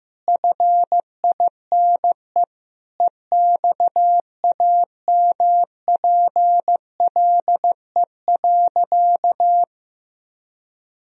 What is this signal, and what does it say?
Morse code, heard as a keyed tone, about 15 wpm, 700 Hz. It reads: FINE EXAMPLE.
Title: Sentences from Top 1000 Words - Review Part 1 - 15wpm